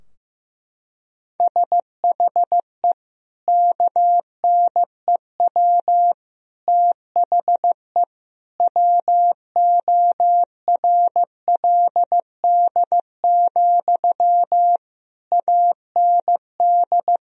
Morse code, heard as a keyed tone, about 15 wpm, 700 Hz. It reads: SHE KNEW THE WORLD, AND
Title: Sentences from Top 1000 Words - Review Part 4 - 15wpm